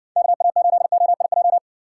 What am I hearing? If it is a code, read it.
Morse code, heard as a keyed tone, about 60 wpm, 700 Hz. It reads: KA8OIJ